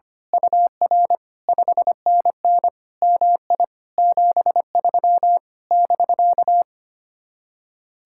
Morse code, heard as a keyed tone, about 24 wpm, 700 Hz. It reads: UR 5NN MI 73 <BK>
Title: Speed-Racing Common QSO Phrases - 25wpm